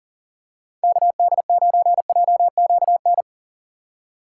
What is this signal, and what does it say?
Morse code, heard as a keyed tone, about 40 wpm, 700 Hz. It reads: KD9JQN
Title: USA Call Signs - 6 Characters - 3x Character Spacing - 40wpm